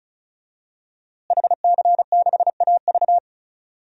Morse code, heard as a keyed tone, about 36 wpm, 700 Hz. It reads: HC6AV
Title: DXCC-100 Call Signs - 5 Characters - Repeated 2x - 35wpm